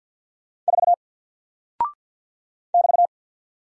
Morse code, heard as a keyed tone, about 48 wpm, 700 Hz. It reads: 4 <BT>
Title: Single Character (Letter:Number:Punctuation:Prosign) Alt 50wpm